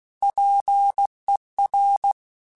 Morse code, heard as a keyed tone, about 16 wpm, 800 Hz. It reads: PER
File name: Technical Subjects / Technical Subjects-mp3s / XC @15